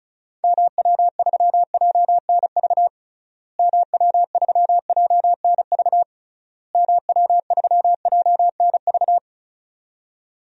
Morse code, heard as a keyed tone, about 34 wpm, 700 Hz. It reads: MW3JNV MW3JNV MW3JNV
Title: DXCC-100 Call Signs - 6 Characters - Repeated 3x - 35wpm